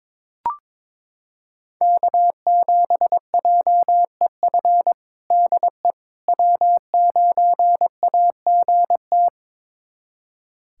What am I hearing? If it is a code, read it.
Morse code, heard as a keyed tone, about 22 wpm, 700 Hz. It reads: K7JEF DE W9AGT
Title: USA Callsign DE Callsign - 5 Characters - 22wpm